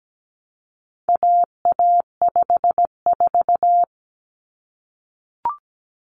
Morse code, heard as a keyed tone, about 17 wpm, 700 Hz. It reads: AA54
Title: Common Cut Numbers - 4-Digit 17wpm